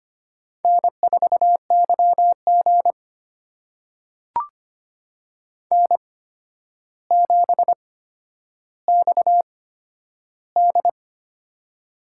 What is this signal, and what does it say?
Morse code, heard as a keyed tone, about 25 wpm, spaced as for 11 wpm, 700 Hz. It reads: N4YG N7XD
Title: USA Call Signs - 4 Characters - 8x Character Spacing - 25wpm